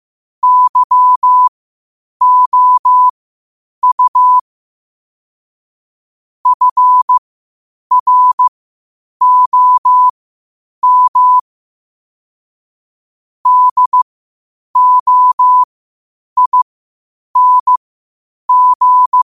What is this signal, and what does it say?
Morse code, heard as a keyed tone, about 15 wpm, spaced as for 8 wpm, 1000 Hz. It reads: YOU FROM DOING
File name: qotdfiles / 8wpm